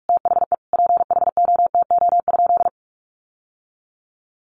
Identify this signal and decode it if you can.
Morse code, heard as a keyed tone, about 44 wpm, 700 Hz. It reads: THE PHOTO?
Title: Sentences from Top 1200 Words 45wpm